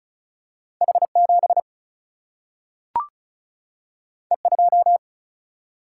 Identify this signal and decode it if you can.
Morse code, heard as a keyed tone, about 36 wpm, 700 Hz. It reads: H7 E2
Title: Single-Letter Single-Number 35wpm